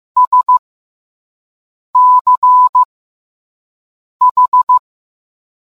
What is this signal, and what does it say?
Morse code, heard as a keyed tone, about 15 wpm, spaced as for 5 wpm, 1000 Hz. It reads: SCH